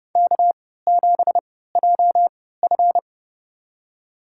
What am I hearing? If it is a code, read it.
Morse code, heard as a keyed tone, about 32 wpm, spaced as for 17 wpm, 700 Hz. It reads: K7JF
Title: USA Call Signs - 4 Characters - 3x Character Spacing - 30wpm